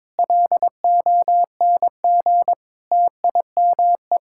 Morse code, heard as a keyed tone, about 22 wpm, 700 Hz. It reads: LONG TIME
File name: Sentences with 7-Word Phrases - 22wpm